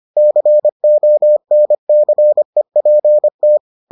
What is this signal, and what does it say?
Morse code, heard as a keyed tone, about 25 wpm, 600 Hz. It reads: CONCEPT